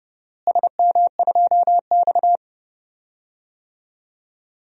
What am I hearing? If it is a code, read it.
Morse code, heard as a keyed tone, about 30 wpm, 700 Hz. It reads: SM2X
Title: DXCC-100 Call Signs - 4 Characters - Repeated 3x - 30wpm